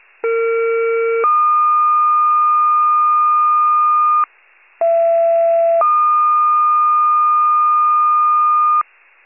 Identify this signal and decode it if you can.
Two-tone page, heard as a one-second tone, then a three-second tone, about 460 then 1220 Hz, about 670 then 1220 Hz. It is heard from 0.2 to 8.8 s.